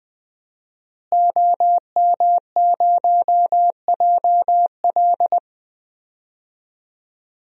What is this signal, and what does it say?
Morse code, heard as a keyed tone, about 21 wpm, 700 Hz. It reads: OM0JL
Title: DXCC-100 Call Signs - 5 Characters - Repeated 3x - 20wpm